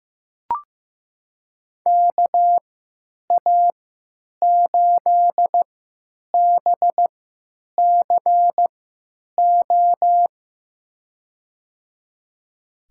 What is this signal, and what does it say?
Morse code, heard as a keyed tone, about 15 wpm, spaced as for 9 wpm, 700 Hz. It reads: KA8BCO
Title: USA Call Signs - 6 Characters - 3x Character Spacing - 15wpm